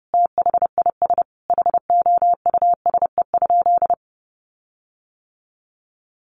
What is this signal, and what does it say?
Morse code, heard as a keyed tone, about 30 wpm, 700 Hz. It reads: THIS HOUSE?
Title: Sentences from Top 1400 Words 30wpm